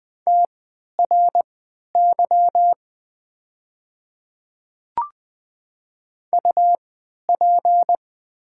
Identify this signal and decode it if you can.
Morse code, heard as a keyed tone, about 20 wpm, spaced as for 11 wpm, 690 Hz. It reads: TRY UP
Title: 3x Character Spacing - Top 300 Words - 20wpm